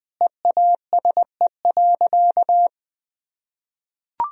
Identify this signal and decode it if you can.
Morse code, heard as a keyed tone, about 20 wpm, 700 Hz. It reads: EASE.
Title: Sentences from Top 1000 Words - Review Part 1 - 20wpm